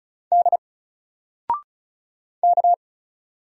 Morse code, heard as a keyed tone, about 33 wpm, 700 Hz. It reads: D K
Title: Single Letter-Number Alt 35wpm